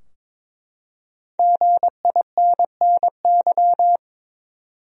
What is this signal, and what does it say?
Morse code, heard as a keyed tone, about 22 wpm, 700 Hz. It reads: GINNY
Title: USA Common Names - Max 5 Letters - 22wpm